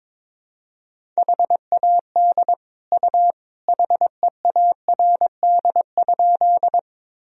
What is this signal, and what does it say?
Morse code, heard as a keyed tone, about 22 wpm, 700 Hz. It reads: HAD U HEARD?
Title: Sentences from Top 400 Words and Abbreviations 22wpm